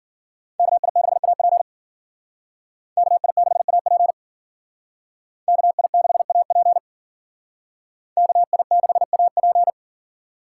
Morse code, heard as a keyed tone, about 52 wpm, 700 Hz. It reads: KI6AP KI6AP KI6AP KI6AP